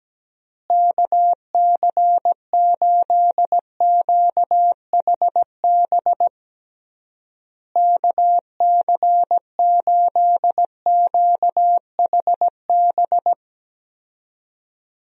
Morse code, heard as a keyed tone, about 17 wpm, 700 Hz. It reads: KC8QHB KC8QHB
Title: USA Call Signs - 6 Characters - Repeated 3x - 17wpm